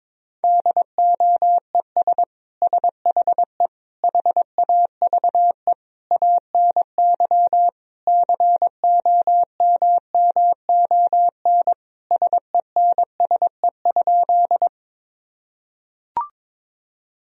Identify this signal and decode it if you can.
Morse code, heard as a keyed tone, about 22 wpm, 700 Hz. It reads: DOES SHE HAVE ANY COMMON SENSE?
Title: Sentences from Top 1000 Words - Review Part 4 - 22wpm